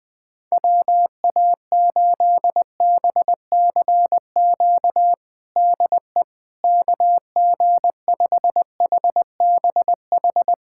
Morse code, heard as a keyed tone, about 21 wpm, 700 Hz. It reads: WA8BCQ DE KG5HBH
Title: USA Callsign DE Callsign - 6 Characters - 20wpm